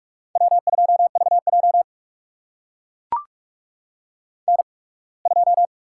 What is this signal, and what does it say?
Morse code, heard as a keyed tone, about 44 wpm, spaced as for 27 wpm, 690 Hz. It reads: W2VJ N2
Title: USA Call Signs - 4 Characters - 8x Character Spacing - 45wpm